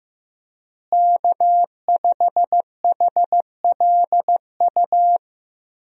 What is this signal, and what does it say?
Morse code, heard as a keyed tone, about 16 wpm, 700 Hz. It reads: K5HLU
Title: USA Call Signs - 5 Characters - Repeated 2x - 15wpm